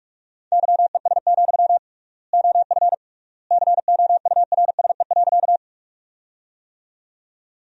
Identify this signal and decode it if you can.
Morse code, heard as a keyed tone, about 46 wpm, 700 Hz. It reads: YES, OF COURSE.
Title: 3x Word-Spacing - Sentences from Top 500 Words - 45wpm